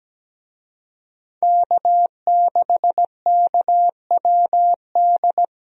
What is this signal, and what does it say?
Morse code, heard as a keyed tone, about 17 wpm, 700 Hz. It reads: K6KWD
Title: USA Call Signs - 5 Characters - Repeated 3x - 17wpm